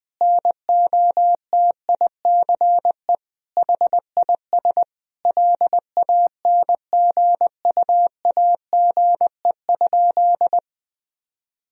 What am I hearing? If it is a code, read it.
Morse code, heard as a keyed tone, about 20 wpm, 700 Hz. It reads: NOTICE HIS LANGUAGE?
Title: Sentences from Top 1000 Words - Review Part 3 - 20wpm